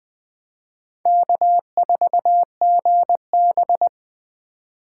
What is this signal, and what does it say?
Morse code, heard as a keyed tone, about 20 wpm, 700 Hz. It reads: K4GB
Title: USA Call Signs - 4 Characters - 4x Character Spacing - 20wpm